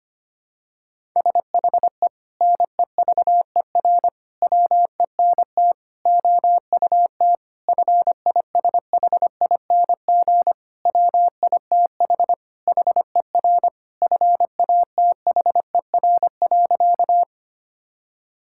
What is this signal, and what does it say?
Morse code, heard as a keyed tone, about 25 wpm, 700 Hz. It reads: SHE NEVER WENT OUT FISHING WITH HER FATHER.
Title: Sentences from Top 1000 Words - Review Part 5 - 25wpm